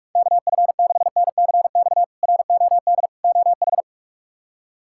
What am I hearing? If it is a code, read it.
Morse code, heard as a keyed tone, about 44 wpm, 700 Hz. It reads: KF6NCX ROD OH
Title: K1USN SST QSO Practice - 45wpm